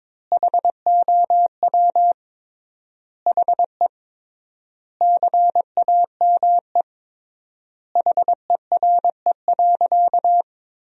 Morse code, heard as a keyed tone, about 21 wpm, 700 Hz. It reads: HOW HE CAME HERE.
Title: 3x Word-Spacing - Sentences from Top 500 Words - 22wpm